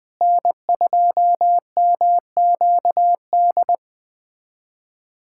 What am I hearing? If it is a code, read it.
Morse code, heard as a keyed tone, about 20 wpm, 700 Hz. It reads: N2MQD
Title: USA Callsign DE Callsign 20wpm